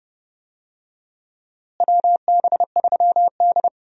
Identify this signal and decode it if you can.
Morse code, heard as a keyed tone, about 30 wpm, 700 Hz. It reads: WB3D
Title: USA Call Signs - 4 Characters - Repeated 3x - 30wpm